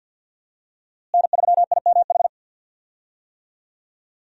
Morse code, heard as a keyed tone, about 48 wpm, 700 Hz. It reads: N3IMH
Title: USA Call Signs - 5 Characters - Repeated 2x - 50wpm